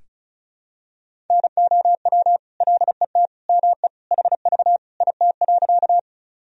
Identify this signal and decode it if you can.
Morse code, heard as a keyed tone, about 36 wpm, 700 Hz. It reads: NOW LET ME HV IT.
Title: Sentences from Top 300 Words and Abbreviations 35wpm